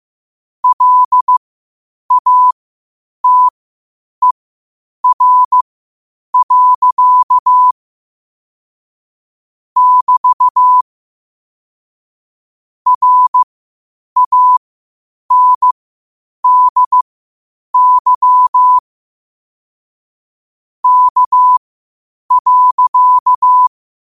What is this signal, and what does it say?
Morse code, heard as a keyed tone, about 15 wpm, spaced as for 8 wpm, 1000 Hz. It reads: LATER. <BT> RANDY K.
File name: qotdfiles / 8wpm